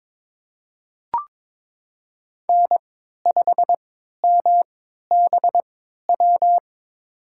Morse code, heard as a keyed tone, about 22 wpm, spaced as for 12 wpm, 700 Hz. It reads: N5MBW
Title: USA Call Signs - 5 Characters - 3x Character Spacing - 22wpm